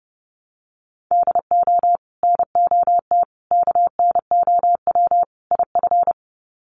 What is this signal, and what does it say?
Morse code, heard as a keyed tone, about 29 wpm, 700 Hz. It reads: DO NOT KNOW IF
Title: Sets of 4 Words 30wpm